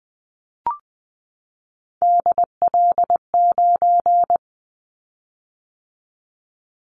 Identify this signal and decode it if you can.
Morse code, heard as a keyed tone, about 20 wpm, 700 Hz. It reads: DL9
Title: Callsign Prefixes for Top 50 DXCC - 20wpm